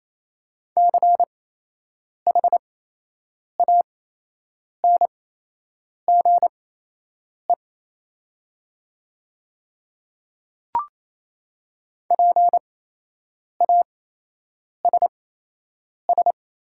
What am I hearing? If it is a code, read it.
Morse code, heard as a keyed tone, about 27 wpm, spaced as for 8 wpm, 700 Hz. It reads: CHANGE PASS
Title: 8x Character Spacing - Top 500 Words - 28wpm